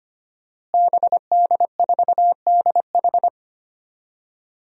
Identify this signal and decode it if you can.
Morse code, heard as a keyed tone, about 25 wpm, 700 Hz. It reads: BD4DH
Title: DXCC-100 Call Signs - 5 Characters - Repeated 2x - 25wpm